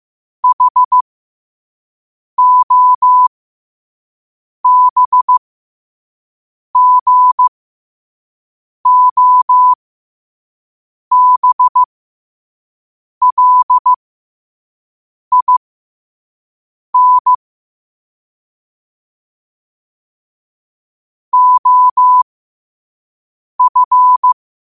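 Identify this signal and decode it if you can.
Morse code, heard as a keyed tone, about 15 wpm, spaced as for 5 wpm, 1000 Hz. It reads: HOBGOBLIN OF
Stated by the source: Morse code quote of the day at 5 WPM.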